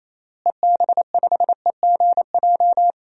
Morse code, heard as a keyed tone, about 28 wpm, 700 Hz. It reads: EB5EGJ
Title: DXCC-100 Call Signs - 6 Characters - Repeated 2x - 28wpm